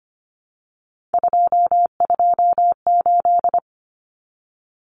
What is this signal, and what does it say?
Morse code, heard as a keyed tone, about 25 wpm, 700 Hz. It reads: 228
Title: Numbers - 3-Digit 25wpm